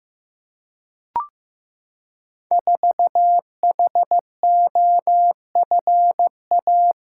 Morse code, heard as a keyed tone, about 15 wpm, 700 Hz. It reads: 4HOFA
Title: Code Groups - 5 Letter:Numbers - 15wpm